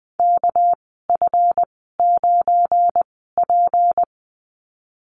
Morse code, heard as a keyed tone, about 20 wpm, spaced as for 15 wpm, 700 Hz. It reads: KF9P
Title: 219 Lesson 35g - Number 0 - Call Signs 20wpm